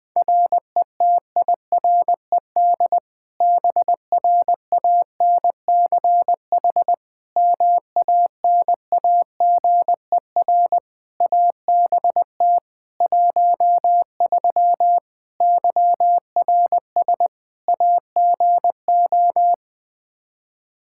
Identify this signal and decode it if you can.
Morse code, heard as a keyed tone, about 20 wpm, 700 Hz. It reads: RETIRED BRANCH MANAGER ABT 13 YRS AGO
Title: K1USN SST QSO Practice 20wpm